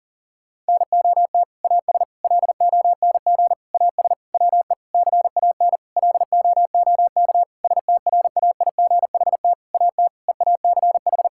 Morse code, heard as a keyed tone, about 40 wpm, 690 Hz. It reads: NOT AS LONG AS WE CAN LOOK STRAIGHT AT EACH
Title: Sentences from Top 1000 Words - Review Part 6 - 40wpm